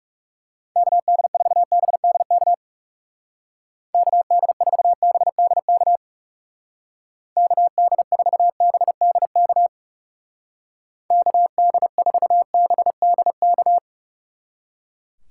Morse code, heard as a keyed tone, about 37 wpm, 700 Hz. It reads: KD4BDK KD4BDK KD4BDK KD4BDK
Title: Speed Racing - USA 2x3 Call Signs 30wpm